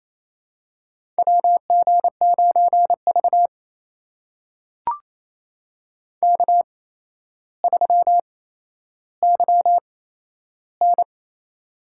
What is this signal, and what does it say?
Morse code, heard as a keyed tone, about 28 wpm, spaced as for 12 wpm, 700 Hz. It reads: WG9V K3YN
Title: USA Call Signs - 4 Characters - 8x Character Spacing - 28wpm